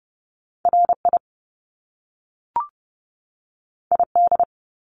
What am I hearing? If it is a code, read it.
Morse code, heard as a keyed tone, about 30 wpm, 700 Hz. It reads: RI ID
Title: Top Two Letter Combinations in Words 30wpm